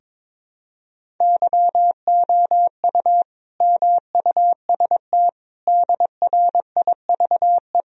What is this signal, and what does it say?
Morse code, heard as a keyed tone, about 22 wpm, 700 Hz. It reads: YOU MUST DRIVE
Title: Sentences from Top 1000 Words - Review Part 2 - 22wpm